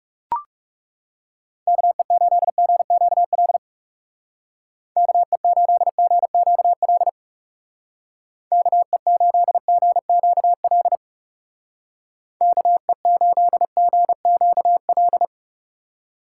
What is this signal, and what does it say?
Morse code, heard as a keyed tone, about 35 wpm, 700 Hz. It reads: KE8GQL KE8GQL KE8GQL KE8GQL
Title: Speed Racing - USA Call Signs 30wpm